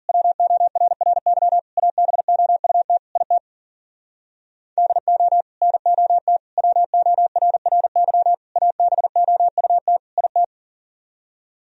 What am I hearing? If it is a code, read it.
Morse code, heard as a keyed tone, about 52 wpm, 700 Hz. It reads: WORRY ABOUT IT DO NOT WORRY ABOUT IT